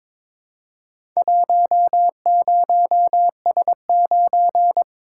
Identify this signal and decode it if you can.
Morse code, heard as a keyed tone, about 23 wpm, 700 Hz. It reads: 10S9
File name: practice / 22wpm / Four Letter-Number Combinations - 22wpm